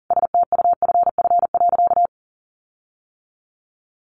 Morse code, heard as a keyed tone, about 40 wpm, 700 Hz. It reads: STUFF.
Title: Sentences from Top 1400 Words 40wpm